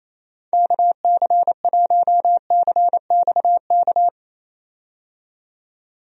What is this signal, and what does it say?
Morse code, heard as a keyed tone, about 28 wpm, 700 Hz. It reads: KC1CXK
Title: USA Call Signs - 6 Characters - 28wpm